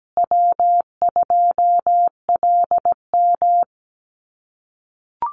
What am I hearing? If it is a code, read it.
Morse code, heard as a keyed tone, about 17 wpm, 700 Hz. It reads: W2LM
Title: USA Call Signs Repeated 3x 17wpm